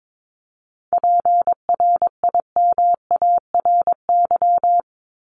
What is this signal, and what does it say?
Morse code, heard as a keyed tone, about 22 wpm, 700 Hz. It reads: PRIMARY
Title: LY Suffix - Top 100 Words with Base Words 22wpm